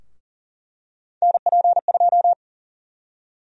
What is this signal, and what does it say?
Morse code, heard as a keyed tone, about 39 wpm, 700 Hz. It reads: NP2
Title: USA Call Sign Prefixes - Repeated 2x - 40wpm